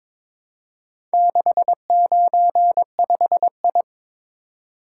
Morse code, heard as a keyed tone, about 22 wpm, 700 Hz. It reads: 695I
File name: practice / 22wpm / Four Letter-Number Combinations - 22wpm